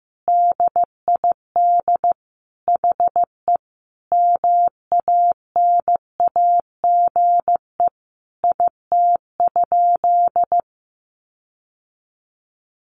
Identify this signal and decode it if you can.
Morse code, heard as a keyed tone, about 15 wpm, 700 Hz. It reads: DID HE MANAGE IT?
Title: Sentences from Top 1400 Words 15wpm